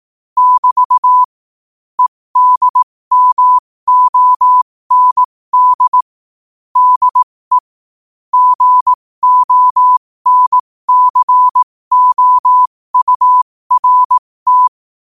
Morse code, heard as a keyed tone, about 18 wpm, spaced as for 15 wpm, 1000 Hz. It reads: <BT> EDMOND DE GONCOURT
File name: qotdfiles / 15wpm